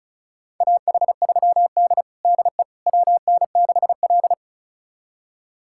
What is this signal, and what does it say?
Morse code, heard as a keyed tone, about 36 wpm, 700 Hz. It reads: AH3D DE WN6L
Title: USA Callsign DE Callsign - 4 Characters - 35wpm